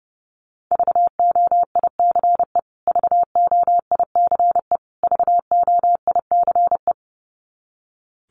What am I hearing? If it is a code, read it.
Morse code, heard as a keyed tone, about 30 wpm, 700 Hz. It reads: VOICE VOICE VOICE
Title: Top 500 Words Repeated 30wpm